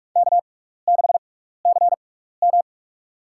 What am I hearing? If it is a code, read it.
Morse code, heard as a keyed tone, about 43 wpm, spaced as for 15 wpm, 700 Hz. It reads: K6CM
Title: USA Call Signs - 4 Characters - 6x Character Spacing - 45wpm